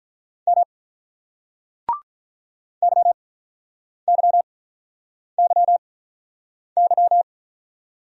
Morse code, heard as a keyed tone, about 44 wpm, spaced as for 8 wpm, 700 Hz. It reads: M YYYY